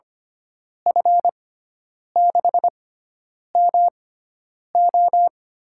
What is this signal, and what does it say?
Morse code, heard as a keyed tone, about 25 wpm, spaced as for 9 wpm, 700 Hz. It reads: F6MO
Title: USA Call Signs - 6 Characters - 6x Character Spacing - 25wpm